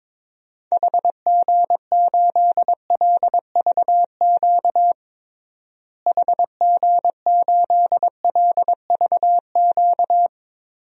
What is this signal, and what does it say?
Morse code, heard as a keyed tone, about 22 wpm, 700 Hz. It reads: HG8LVQ HG8LVQ
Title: DXCC-100 Call Signs - 6 Characters - Repeated 3x - 22wpm